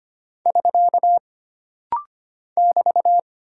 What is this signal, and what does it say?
Morse code, heard as a keyed tone, about 25 wpm, 700 Hz. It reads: <SK> <BT>
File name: Single Punctuation-Prosign - Rapid-Fire - 25wpm